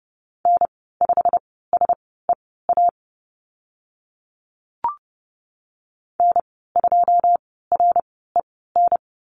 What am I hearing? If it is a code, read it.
Morse code, heard as a keyed tone, about 30 wpm, spaced as for 17 wpm, 700 Hz. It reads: N5SEA N2REN
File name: course / 30:10wpm / 088 Lesson 15g - Number 5 - Call Signs 30wpm